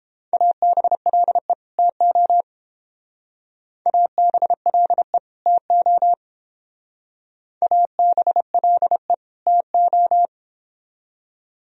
Morse code, heard as a keyed tone, about 29 wpm, 700 Hz. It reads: ABLE TO ABLE TO ABLE TO